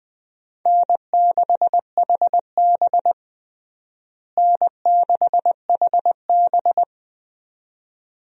Morse code, heard as a keyed tone, about 20 wpm, 700 Hz. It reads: N6HB N6HB
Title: USA Call Signs - 4 Characters - Repeated 2x - 20wpm